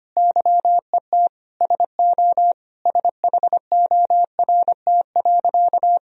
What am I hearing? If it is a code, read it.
Morse code, heard as a keyed tone, about 25 wpm, 700 Hz. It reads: YET SO SHORT.
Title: Sentences from Top 500 Words and Abbreviations 25wpm